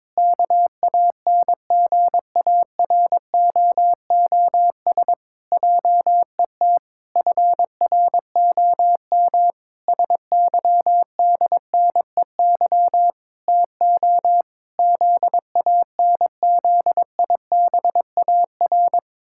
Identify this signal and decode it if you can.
Morse code, heard as a keyed tone, about 22 wpm, 700 Hz. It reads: KANGAROOS JET FROM SYDNEY TO ZANZIBAR
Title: Pangram Challenge v2 - 22wpm